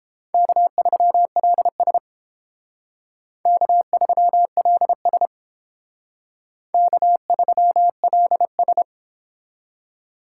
Morse code, heard as a keyed tone, about 32 wpm, 700 Hz. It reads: K3LS K3LS K3LS